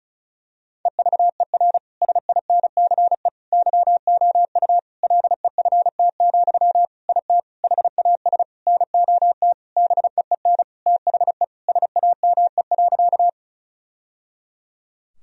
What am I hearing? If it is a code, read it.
Morse code, heard as a keyed tone, about 34 wpm, 700 Hz. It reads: EVER SINCE YOU LEFT, IT HAS NOT BEEN THE SAME.
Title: Sentences from Top 1000 Words - Review Part 1 - 35wpm